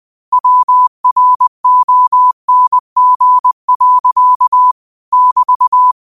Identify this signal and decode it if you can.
Morse code, heard as a keyed tone, about 20 wpm, 1000 Hz. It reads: WRONG. <BT>
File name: qotdfiles / 20wpm